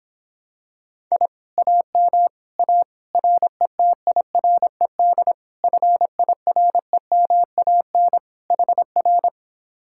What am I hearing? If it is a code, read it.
Morse code, heard as a keyed tone, about 26 wpm, 700 Hz. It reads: I AM A RETIRED FIREMAN HR